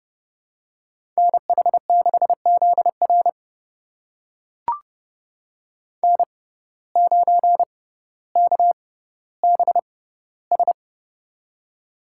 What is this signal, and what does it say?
Morse code, heard as a keyed tone, about 30 wpm, spaced as for 15 wpm, 700 Hz. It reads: NH6ZR N9KBS
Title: USA Call Signs - 5 Characters - 6x Character Spacing - 30wpm